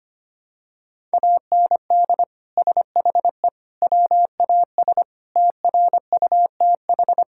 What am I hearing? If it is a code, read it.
Morse code, heard as a keyed tone, about 25 wpm, 700 Hz. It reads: AND SHE WAS TRUTH
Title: Sentences from Top 1000 Words - Review Part 5 - 25wpm